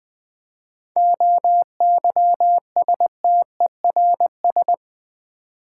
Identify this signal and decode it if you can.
Morse code, heard as a keyed tone, about 20 wpm, 700 Hz. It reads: OYSTERS
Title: Difficult Words with Hints - 20wpm